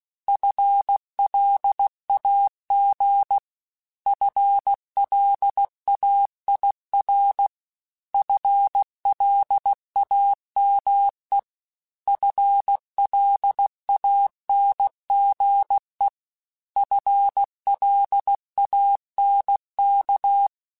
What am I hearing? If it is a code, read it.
Morse code, heard as a keyed tone, about 16 wpm, 800 Hz. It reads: FLAG FLAIR FLAME FLANGE FLANK
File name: FL Runs / FL Runs mp3s / FL Runs @15